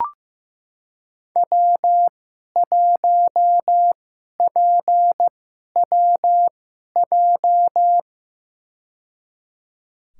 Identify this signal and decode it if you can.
Morse code, heard as a keyed tone, about 15 wpm, spaced as for 11 wpm, 700 Hz. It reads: W1PWJ
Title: USA Call Signs - 5 Characters - 2x Character Spacing - 15wpm